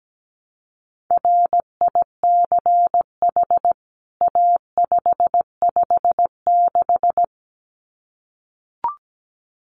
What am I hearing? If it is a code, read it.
Morse code, heard as a keyed tone, about 17 wpm, 700 Hz. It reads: RICH A556
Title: CWT Exchange 17wpm